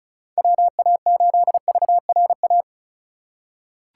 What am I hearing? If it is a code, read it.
Morse code, heard as a keyed tone, about 36 wpm, 700 Hz. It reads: WA8VRA
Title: USA Call Signs - 6 Characters - 3x Character Spacing - 35wpm